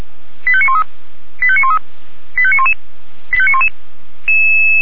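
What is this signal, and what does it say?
Five-tone selective call (ZVEI), heard as quick runs of five tones, tones of about 70 ms. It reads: 85913 85913 85911 85911
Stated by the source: ZVEI five-tone call received on SDR.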